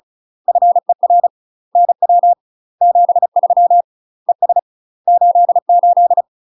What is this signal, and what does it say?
Morse code, heard as a keyed tone, about 37 wpm, 700 Hz. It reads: FER NW 73 ES 88
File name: UK MASTER QSOs TRAINING / 35WPM / UK MASTER QSOs 0001